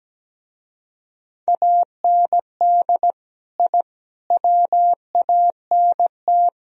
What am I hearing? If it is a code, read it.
Morse code, heard as a keyed tone, about 18 wpm, 700 Hz. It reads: AND I WANT
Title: Speed-Racing - Sets of 4 Words 17wpm